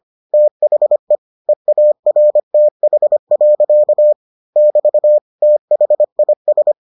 Morse code, heard as a keyed tone, about 25 wpm, 600 Hz. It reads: THE EARTH. <BT> THIS